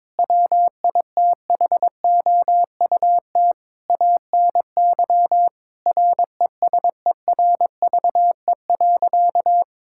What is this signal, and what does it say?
Morse code, heard as a keyed tone, about 22 wpm, 700 Hz. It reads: WITHOUT ANY RESERVE.
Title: Sentences from Top 1000 Words - Review Part 6 - 22wpm